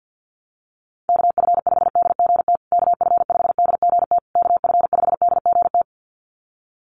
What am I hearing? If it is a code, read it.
Morse code, heard as a keyed tone, about 50 wpm, 700 Hz. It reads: KF5DGT KF5DGT KF5DGT
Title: USA Call Signs Repeated 3x 50wpm